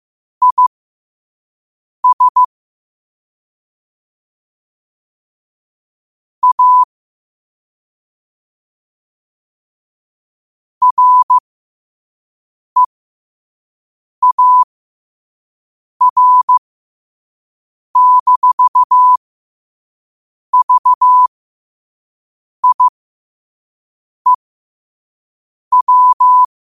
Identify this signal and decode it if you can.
Morse code, heard as a keyed tone, about 15 wpm, spaced as for 5 wpm, 1000 Hz. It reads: IS A REAR-VIEW
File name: qotdfiles / 5wpm